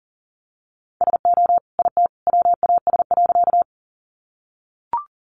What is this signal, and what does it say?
Morse code, heard as a keyed tone, about 40 wpm, 700 Hz. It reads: SO IT WAS.
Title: Sentences from Top 100 Words 40wpm